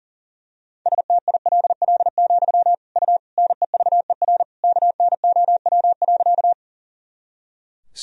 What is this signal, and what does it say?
Morse code, heard as a keyed tone, about 40 wpm, 700 Hz. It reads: STILL, U NEVER KNOW.
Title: Sentences from Top 300 Words and Abbreviations 40wpm